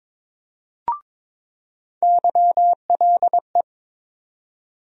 Morse code, heard as a keyed tone, about 22 wpm, 700 Hz. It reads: YLE
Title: USA Call Sign Postfixes - 3 Letters 22wpm